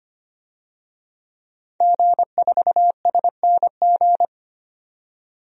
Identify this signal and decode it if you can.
Morse code, heard as a keyed tone, about 25 wpm, 700 Hz. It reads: G4SNG
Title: DXCC-100 Call Signs - 5 Characters - 25wpm